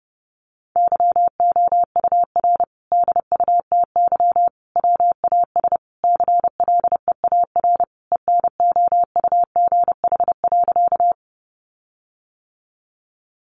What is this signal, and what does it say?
Morse code, heard as a keyed tone, about 28 wpm, 700 Hz. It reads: YOUR DUTY WAS CLEAR ENOUGH.
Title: Sentences from Top 1500 Words 30wpm